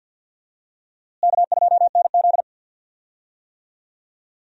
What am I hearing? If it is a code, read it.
Morse code, heard as a keyed tone, about 52 wpm, 700 Hz. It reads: K2NZ
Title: USA Call Signs - 4 Characters - Repeated 3x - 50wpm